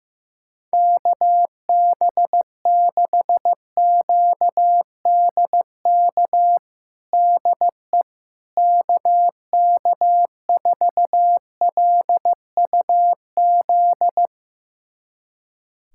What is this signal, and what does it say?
Morse code, heard as a keyed tone, about 15 wpm, 700 Hz. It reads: KB6QDK DE KK4LUZ
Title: USA Callsign DE Callsign 15wpm